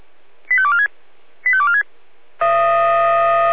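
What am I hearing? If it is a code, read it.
Five-tone selective call (ZVEI), heard as quick runs of five tones, tones of about 70 ms. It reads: 85246 85246
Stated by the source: ZVEI five-tone selective call received by SDR.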